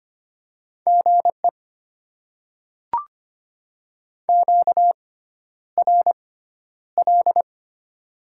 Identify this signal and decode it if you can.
Morse code, heard as a keyed tone, about 25 wpm, spaced as for 11 wpm, 700 Hz. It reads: GE QRL
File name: Top 20 QSO Elements - 6x Character Spacing - 25wpm